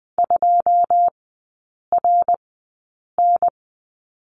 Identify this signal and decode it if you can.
Morse code, heard as a keyed tone, about 20 wpm, spaced as for 8 wpm, 700 Hz. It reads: 2RN
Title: 087 Lesson 15e - Number 5 - Three Characters 20wpm